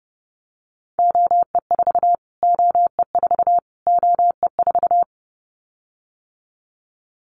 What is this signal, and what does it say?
Morse code, heard as a keyed tone, about 30 wpm, 700 Hz. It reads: OE4 OE4 OE4
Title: Callsign Prefixes for Top 25 DXCC - Repeated 3x - 30wpm